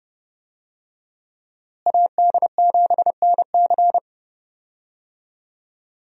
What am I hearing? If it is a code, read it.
Morse code, heard as a keyed tone, about 32 wpm, 700 Hz. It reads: AD7NC
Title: USA Call Signs - 5 Characters - Repeated 3x - 30wpm